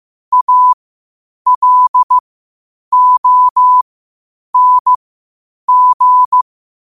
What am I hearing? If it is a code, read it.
Morse code, heard as a keyed tone, about 15 wpm, spaced as for 8 wpm, 1000 Hz. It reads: ALONG
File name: qotdfiles / 8wpm